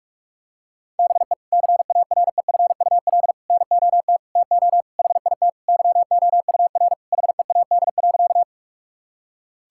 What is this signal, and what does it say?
Morse code, heard as a keyed tone, about 42 wpm, 700 Hz. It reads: BE CAREFUL NOT TO HIT YOUR HEAD.
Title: Sentences from Top 1600 Words 45wpm